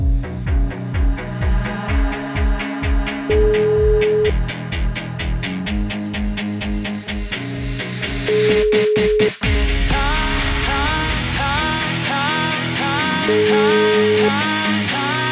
Ringback tone (Europe, 425 Hz), with noise.